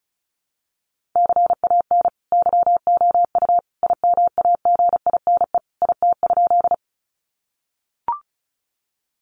Morse code, heard as a keyed tone, about 34 wpm, 700 Hz. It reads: CAN YOU IMAGINE IT?
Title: Sentences from Top 1400 Words 35wpm